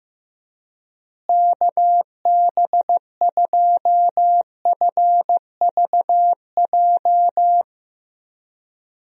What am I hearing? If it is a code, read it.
Morse code, heard as a keyed tone, about 15 wpm, 700 Hz. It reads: KB2FVJ
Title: USA Call Signs - 6 Characters - Repeated 2x - 15wpm